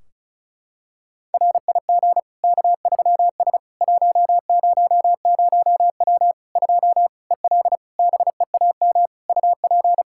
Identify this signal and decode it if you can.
Morse code, heard as a keyed tone, about 36 wpm, 700 Hz. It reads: RIG K3S 100W 2 EL BEAM UP
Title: Practice QSO - Exchange 2 - Info - 35wpm